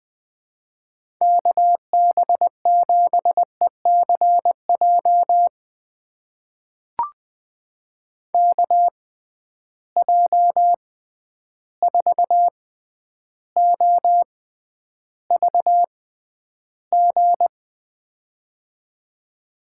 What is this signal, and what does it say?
Morse code, heard as a keyed tone, about 20 wpm, spaced as for 10 wpm, 700 Hz. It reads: KB7ECJ KJ4OVG
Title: USA Call Signs - 6 Characters - 6x Character Spacing - 20wpm